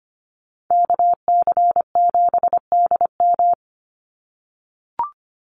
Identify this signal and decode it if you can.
Morse code, heard as a keyed tone, about 25 wpm, 700 Hz. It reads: KC7DM
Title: NA SOTA Activator Callsigns 25wpm